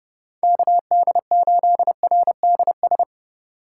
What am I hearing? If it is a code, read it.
Morse code, heard as a keyed tone, about 36 wpm, 700 Hz. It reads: KD8RDS
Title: Speed Racing - USA 2x3 Call Signs 30wpm